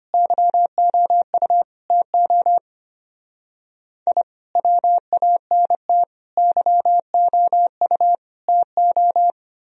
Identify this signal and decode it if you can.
Morse code, heard as a keyed tone, about 32 wpm, 690 Hz. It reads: YOU TO I WANT YOU TO